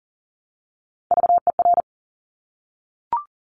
Morse code, heard as a keyed tone, about 41 wpm, 700 Hz. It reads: VER
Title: Top Three Letter Combinations in Words 40wpm